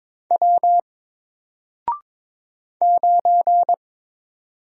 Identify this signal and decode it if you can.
Morse code, heard as a keyed tone, about 22 wpm, 700 Hz. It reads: W 9
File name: Single-Letter-Number-22wpm